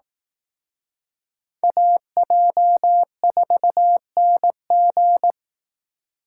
Morse code, heard as a keyed tone, about 19 wpm, 700 Hz. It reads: AJ4NG
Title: Speed Racing - USA 2x2 Call Signs 15wpm